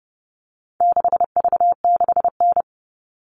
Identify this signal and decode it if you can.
Morse code, heard as a keyed tone, about 29 wpm, 700 Hz. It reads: 6V6N
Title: All Cut Numbers - 4-Digit 30wpm